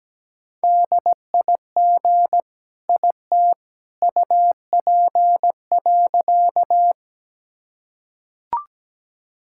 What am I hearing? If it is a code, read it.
Morse code, heard as a keyed tone, about 17 wpm, 700 Hz. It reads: DIG IT UP.
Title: Sentences from Top 1600 Words 17wpm